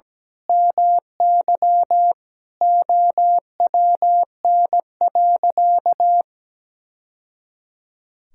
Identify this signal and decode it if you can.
Morse code, heard as a keyed tone, about 17 wpm, 700 Hz. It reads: MY OWN.
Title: Sentences from Top 1000 Words - Review Part 6 - 17wpm